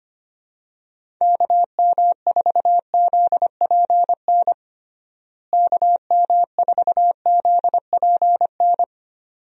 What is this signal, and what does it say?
Morse code, heard as a keyed tone, about 25 wpm, 700 Hz. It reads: KM4ZPN KM4ZPN
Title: USA Call Signs - 6 Characters - Repeated 2x - 25wpm